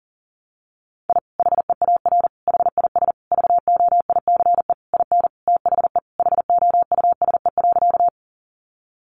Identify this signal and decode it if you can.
Morse code, heard as a keyed tone, about 40 wpm, 700 Hz. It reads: I HEAR HIS VOICE IN THE HOUSE.
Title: Sentences from Top 500 Words 40wpm